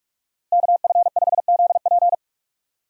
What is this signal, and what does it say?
Morse code, heard as a keyed tone, about 42 wpm, 700 Hz. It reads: KF5ZP
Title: USA Call Signs - 5 Characters - Repeated 3x - 45wpm